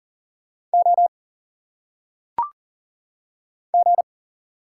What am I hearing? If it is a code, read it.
Morse code, heard as a keyed tone, about 40 wpm, 700 Hz. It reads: O G
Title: Single Letters - Phonetic Pronunciation 40wpm